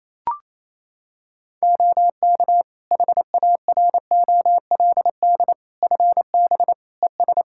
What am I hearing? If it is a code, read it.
Morse code, heard as a keyed tone, about 28 wpm, 700 Hz. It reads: OK HAROLD FB ES
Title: Practice QSO - Exchange 2 - Info - 28wpm